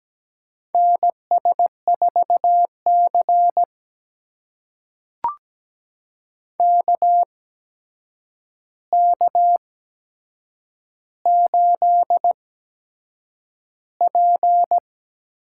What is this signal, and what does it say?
Morse code, heard as a keyed tone, about 17 wpm, spaced as for 7 wpm, 700 Hz. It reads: NS4C KK8P
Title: USA Call Signs - 4 Characters - 8x Character Spacing - 17wpm